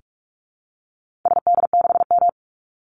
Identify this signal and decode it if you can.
Morse code, heard as a keyed tone, about 44 wpm, 700 Hz. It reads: SD6M
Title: Call Signs - 45wpm